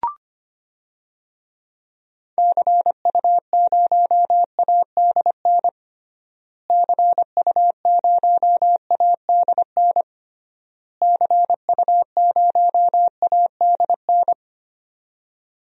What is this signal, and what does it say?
Morse code, heard as a keyed tone, about 25 wpm, 700 Hz. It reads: CU0ADN CU0ADN CU0ADN
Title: DXCC-100 Call Signs - 6 Characters - Repeated 3x - 25wpm